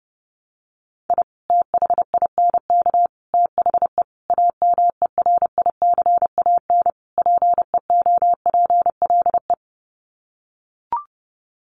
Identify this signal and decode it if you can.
Morse code, heard as a keyed tone, about 30 wpm, 700 Hz. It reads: I THINK THE AMERICAN PEOPLE
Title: Sets of 5 Words Encore 30wpm